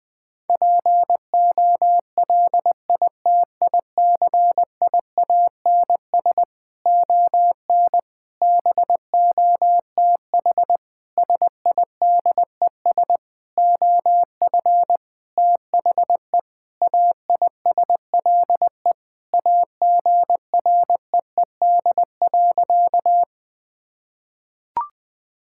Morse code, heard as a keyed tone, about 20 wpm, 700 Hz. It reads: POLITICIANS ON BOTH SIDES OF THE AISLE AGREED.
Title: Sentences with 6-Word Phrases - 20wpm